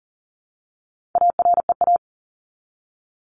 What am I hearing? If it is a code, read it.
Morse code, heard as a keyed tone, about 40 wpm, 700 Hz. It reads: AREA
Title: ES:S Suffix - Top 100 Words with Base Words 40wpm